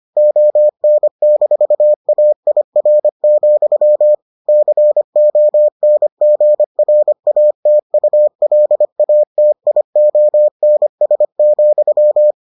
Morse code, heard as a keyed tone, about 25 wpm, 600 Hz. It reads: ON-AIR, CONGRATULATIONS,